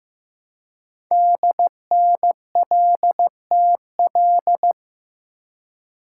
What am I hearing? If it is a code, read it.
Morse code, heard as a keyed tone, about 15 wpm, 700 Hz. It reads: DNLTL
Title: Code Groups - 5 Letters - 15wpm